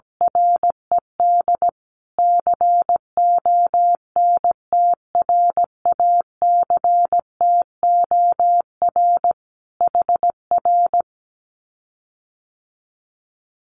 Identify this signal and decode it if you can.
Morse code, heard as a keyed tone, about 17 wpm, 700 Hz. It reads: RED CONTRACTOR HR
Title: Common QSO Phrases Encore 17wpm